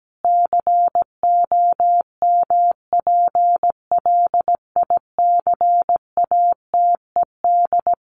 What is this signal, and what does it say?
Morse code, heard as a keyed tone, about 17 wpm, 700 Hz. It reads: COMPLICATED
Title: Sentences from Top 1200 Words 17wpm